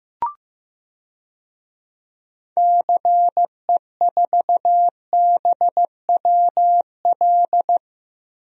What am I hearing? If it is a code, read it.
Morse code, heard as a keyed tone, about 16 wpm, 700 Hz. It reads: CE4BWL
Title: DXCC-100 Call Signs - 6 Characters - Repeated 2x - 15wpm